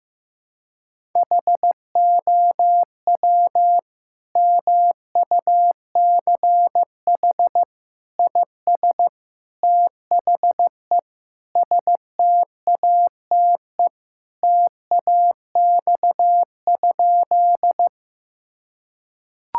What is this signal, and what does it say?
Morse code, heard as a keyed tone, about 15 wpm, 700 Hz. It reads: HOW MUCH IS THE STATE TAX?
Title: Sentences from Top 1000 Words - Review Part 5 - 15wpm